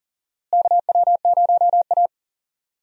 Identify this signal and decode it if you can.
Morse code, heard as a keyed tone, about 40 wpm, 700 Hz. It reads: KW0A
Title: USA Call Signs - 4 Characters - Repeated 3x - 40wpm